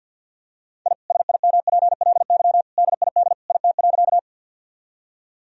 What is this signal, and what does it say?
Morse code, heard as a keyed tone, about 49 wpm, 700 Hz. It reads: I SIMPLY DID IT.
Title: Sentences from Top 1000 Words - Review Part 4 - 50wpm